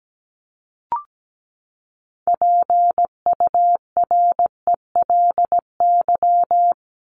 Morse code, heard as a keyed tone, about 17 wpm, 700 Hz. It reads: PURELY
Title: LY Suffix - Top 100 Words with Base Words 17wpm